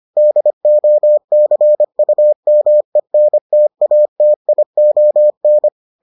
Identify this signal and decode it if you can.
Morse code, heard as a keyed tone, about 25 wpm, 600 Hz. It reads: DOCUMENTATION